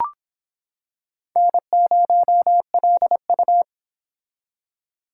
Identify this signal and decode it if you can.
Morse code, heard as a keyed tone, about 25 wpm, 700 Hz. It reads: N0LU